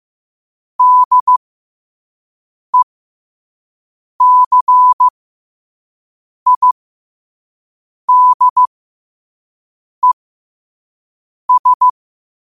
Morse code, heard as a keyed tone, about 15 wpm, spaced as for 5 wpm, 1000 Hz. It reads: DECIDES
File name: qotdfiles / 5wpm